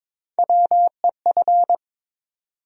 Morse code, heard as a keyed tone, about 22 wpm, 700 Hz. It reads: WEF
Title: USA Call Sign Postfixes - 3 Letters 22wpm